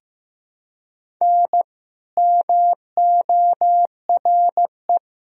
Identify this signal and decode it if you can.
Morse code, heard as a keyed tone, about 15 wpm, 700 Hz. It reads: N MORE
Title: Sentences from Top 400 Words and Abbreviations 15wpm